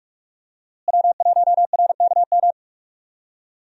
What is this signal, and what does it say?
Morse code, heard as a keyed tone, about 45 wpm, 700 Hz. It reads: W1RKM
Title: USA Call Signs - 5 Characters - Repeated 2x - 45wpm